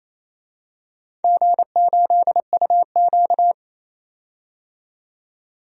Morse code, heard as a keyed tone, about 28 wpm, 700 Hz. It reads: G8UQ
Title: DXCC-100 Call Signs - 4 Characters - 28wpm